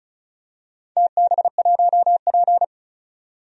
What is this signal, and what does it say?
Morse code, heard as a keyed tone, about 36 wpm, 700 Hz. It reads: TB1P
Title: DXCC-100 Call Signs - 4 Characters - 35wpm